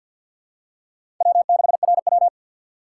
Morse code, heard as a keyed tone, about 51 wpm, 700 Hz. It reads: W6RW
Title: USA Call Signs - 4 Characters - 6x Character Spacing - 50wpm